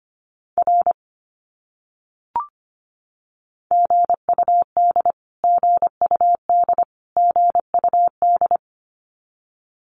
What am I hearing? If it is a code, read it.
Morse code, heard as a keyed tone, about 25 wpm, 700 Hz. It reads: R GUD GUD GUD
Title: Abbreviations Repeated 3x - 25wpm